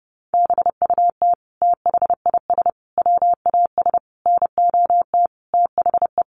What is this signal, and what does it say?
Morse code, heard as a keyed tone, about 30 wpm, 700 Hz. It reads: BUT THIS WAS NOT THE
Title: Sentences from Top 700 Words 30wpm